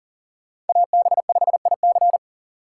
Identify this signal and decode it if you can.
Morse code, heard as a keyed tone, about 50 wpm, 700 Hz. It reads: AB5IC
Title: Speed Racing - USA 2x2 Call Signs 40wpm